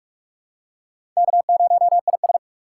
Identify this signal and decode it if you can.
Morse code, heard as a keyed tone, about 46 wpm, 700 Hz. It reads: K0IS
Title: USA Call Signs - 4 Characters - 6x Character Spacing - 45wpm